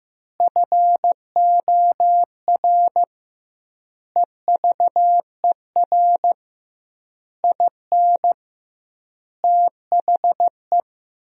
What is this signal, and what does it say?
Morse code, heard as a keyed tone, about 15 wpm, 700 Hz. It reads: FOR EVER IN THE
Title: 2x Word-Spacing - Sentences from Top 400 Words - 15wpm